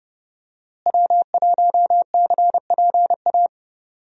Morse code, heard as a keyed tone, about 30 wpm, 700 Hz. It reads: W1CPA
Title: USA Call Signs - 5 Characters - Repeated 2x - 30wpm